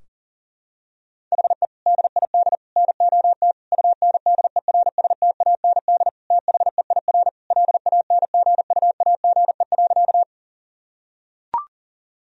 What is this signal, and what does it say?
Morse code, heard as a keyed tone, about 37 wpm, 700 Hz. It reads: HE DID NOT UNDERSTAND THEIR LANGUAGE.
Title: Sentences from Top 1000 Words - Review Part 3 - 40wpm